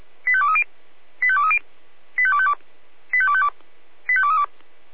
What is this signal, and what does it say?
Five-tone selective call (ZVEI), heard as quick runs of five tones, tones of about 70 ms. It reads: 85230 85230 85252 85252 85232
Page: SDR/ZVEI